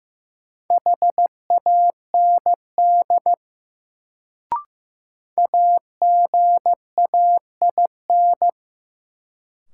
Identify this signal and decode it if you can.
Morse code, heard as a keyed tone, about 15 wpm, 700 Hz. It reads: HAND AGAIN
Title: Sentences from Top 200 Words - New Word Drill - 15wpm